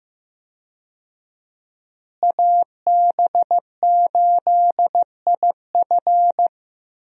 Morse code, heard as a keyed tone, about 15 wpm, 700 Hz. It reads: AB8IF
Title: USA Call Signs - 5 Characters - Repeated 2x - 15wpm